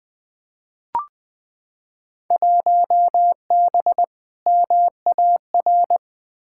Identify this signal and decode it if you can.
Morse code, heard as a keyed tone, about 21 wpm, 700 Hz. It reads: 1B MAR
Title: Field Day Exchanges Encore - 20wpm